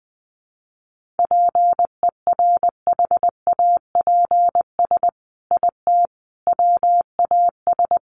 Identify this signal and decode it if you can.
Morse code, heard as a keyed tone, about 20 wpm, 700 Hz. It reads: PERHAPS IT WAS
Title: Sentences from Top 1400 Words 20wpm